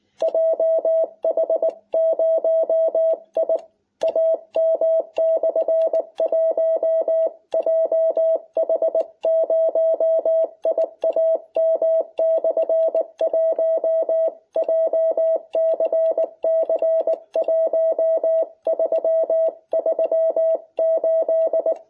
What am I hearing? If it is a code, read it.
Morse code, heard as a keyed tone, about 20 wpm, 660 Hz. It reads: JH0IAM/1 JH0IAM/1 JCC 1338